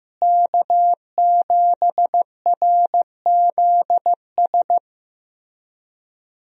Morse code, heard as a keyed tone, about 15 wpm, 700 Hz. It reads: K7RZS
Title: USA Call Signs - 5 Characters - Repeated 3x - 15wpm